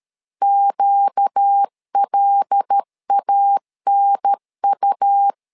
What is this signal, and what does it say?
Morse code, heard as a keyed tone, about 13 wpm, 800 Hz. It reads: QLANU
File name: LEZIONI / 5 LBVC / 5 L 12wpm 1913